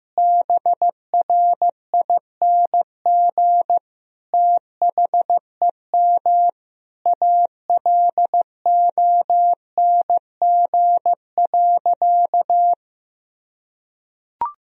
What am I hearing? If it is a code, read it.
Morse code, heard as a keyed tone, about 15 wpm, 700 Hz. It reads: BRING THEM ALONG.
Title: Sentences from Top 1000 Words - Review Part 3 - 15wpm